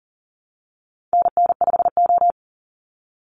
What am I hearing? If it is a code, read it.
Morse code, heard as a keyed tone, about 40 wpm, 700 Hz. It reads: NN5O
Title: Call Signs Repeated 2x - 40wpm